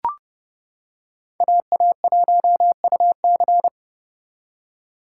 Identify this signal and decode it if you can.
Morse code, heard as a keyed tone, about 31 wpm, 700 Hz. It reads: AA1UC
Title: K1USN SST QSO Practice - 30wpm